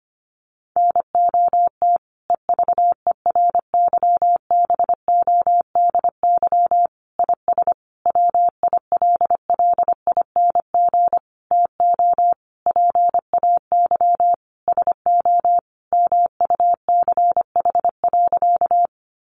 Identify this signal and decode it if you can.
Morse code, heard as a keyed tone, about 25 wpm, 700 Hz. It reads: NOT EVERYBODY IS WILLING TO PAY SO MUCH.
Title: Sentences from Top 1400 Words 25wpm